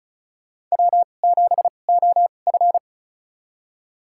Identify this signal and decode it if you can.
Morse code, heard as a keyed tone, about 36 wpm, spaced as for 25 wpm, 700 Hz. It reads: W7OF
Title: USA Call Signs - 4 Characters - 2x Character Spacing - 35wpm